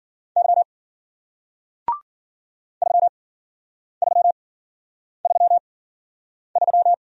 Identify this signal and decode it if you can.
Morse code, heard as a keyed tone, about 55 wpm, spaced as for 9 wpm, 700 Hz. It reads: Y 3333